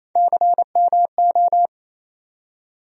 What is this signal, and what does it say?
Morse code, heard as a keyed tone, about 27 wpm, 700 Hz. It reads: CMO
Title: USA Call Sign Postfixes - 3 Letters 28wpm